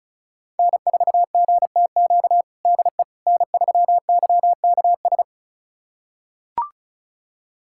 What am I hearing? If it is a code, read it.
Morse code, heard as a keyed tone, about 34 wpm, 690 Hz. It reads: N4GTQ DE N3YKS
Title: USA Callsign DE Callsign - 5 Characters - 35wpm